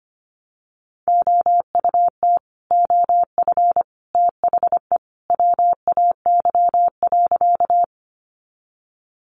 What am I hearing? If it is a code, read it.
Morse code, heard as a keyed tone, about 25 wpm, 700 Hz. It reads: OUT OF THE WAY.
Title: Sentences from Top 100 Words 25wpm